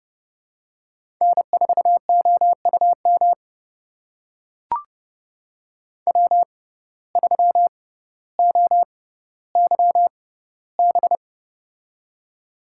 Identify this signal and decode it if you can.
Morse code, heard as a keyed tone, about 30 wpm, spaced as for 15 wpm, 700 Hz. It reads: N4OUM W3OYB
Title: USA Call Signs - 5 Characters - 6x Character Spacing - 30wpm